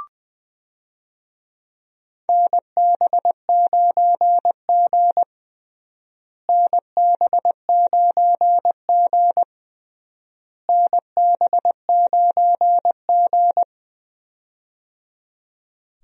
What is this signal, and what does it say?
Morse code, heard as a keyed tone, about 20 wpm, 700 Hz. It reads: NB9G NB9G NB9G
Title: USA Call Signs - 4 Characters - Repeated 3x - 20wpm